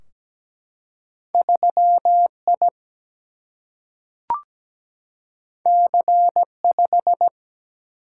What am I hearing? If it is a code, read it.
Morse code, heard as a keyed tone, about 17 wpm, 700 Hz. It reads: 3 I C 5
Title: Single-Letter Single-Number 17wpm